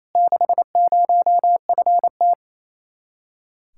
Morse code, heard as a keyed tone, about 30 wpm, 700 Hz. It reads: 60FT
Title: Speed-Racing Common QSO Phrases - 28wpm